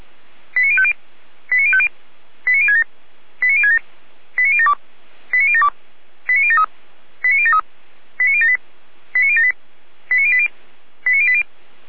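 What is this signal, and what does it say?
Five-tone selective call (ZVEI), heard as quick runs of five tones, tones of about 70 ms. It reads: 89050 89050 89067 89067 89072 89072 89073 89073 89078 89078 89080 89080
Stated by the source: ZVEI five-tone call 89765 received by SDR on channel 455, recorded by fire brigade Feuerwehr Orlamünde.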